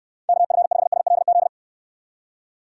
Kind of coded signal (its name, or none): Morse code